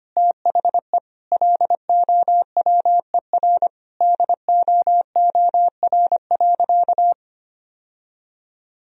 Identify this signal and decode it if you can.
Morse code, heard as a keyed tone, about 25 wpm, 700 Hz. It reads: THE LOWER DOOR.
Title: Sentences from Top 1000 Words - Review Part 4 - 25wpm